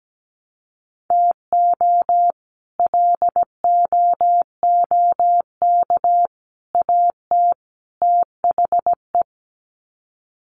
Morse code, heard as a keyed tone, about 17 wpm, 700 Hz. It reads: TO LOOK AT THE
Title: Sets of 4 Words 17wpm